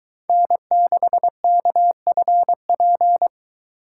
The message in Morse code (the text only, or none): N6KFP